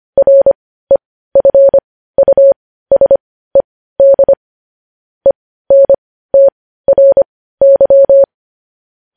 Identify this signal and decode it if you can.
Morse code, heard as a keyed tone, about 25 wpm, spaced as for 15 wpm, 550 Hz. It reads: REFUSED ENTRY
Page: News Headlines in Morse Code at 15 WPM